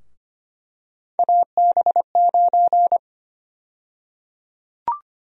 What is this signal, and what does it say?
Morse code, heard as a keyed tone, about 25 wpm, 700 Hz. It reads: AB9
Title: USA Call Sign Prefixes - 25wpm